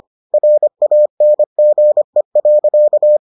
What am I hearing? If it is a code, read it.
Morse code, heard as a keyed tone, about 25 wpm, 600 Hz. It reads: RANGE.